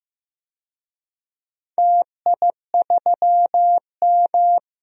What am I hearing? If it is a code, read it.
Morse code, heard as a keyed tone, about 15 wpm, 700 Hz. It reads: TI3M
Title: DXCC-100 Call Signs - 4 Characters - 15wpm